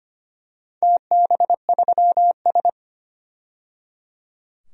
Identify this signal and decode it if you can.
Morse code, heard as a keyed tone, about 25 wpm, 700 Hz. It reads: TB3S
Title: DXCC-100 Call Signs - 4 Characters - 25wpm